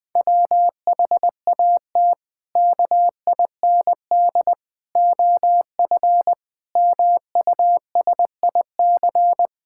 Morse code, heard as a keyed tone, about 21 wpm, 700 Hz. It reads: WHAT KIND OF MUSIC
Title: Sentences with 3-Word Phrases - Spoken Beginning - 20wpm